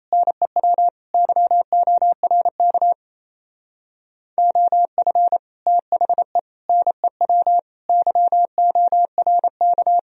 Morse code, heard as a keyed tone, about 35 wpm, 700 Hz. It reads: NEW YORK OF THE NEW YORK